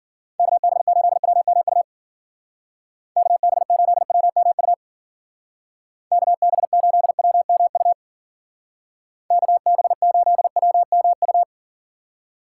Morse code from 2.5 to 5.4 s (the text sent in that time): KB8WMU